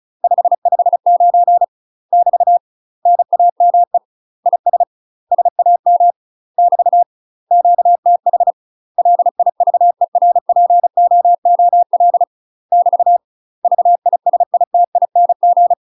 Morse code, heard as a keyed tone, about 35 wpm, 700 Hz. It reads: 559 <BT> NAME IS SAM <BT> QTH LIVERPOOL <BT> VISITING
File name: UK MASTER QSOs TRAINING / 35WPM / UK MASTER QSOs 0003